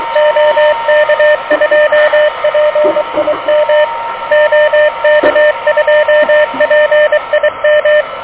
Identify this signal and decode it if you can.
Morse code, heard as a keyed tone, about 24 wpm, 590 Hz, with radio noise. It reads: OK2PIM OK2PIM
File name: OK QRP 2023 / OK QRP 2023